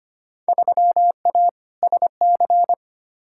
Morse code, heard as a keyed tone, about 25 wpm, 700 Hz. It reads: 3A SC
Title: Field Day Exchanges - 25wpm